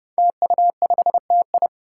Morse code, heard as a keyed tone, about 30 wpm, 700 Hz. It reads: TU5TI
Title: Five Challenging Letter-Number Combinations 30wpm